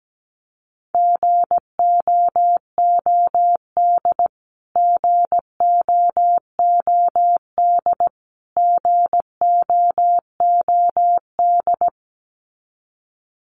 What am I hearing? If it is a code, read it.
Morse code, heard as a keyed tone, about 17 wpm, 700 Hz. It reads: GOOD GOOD GOOD
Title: Top 200 Words Repeated 17wpm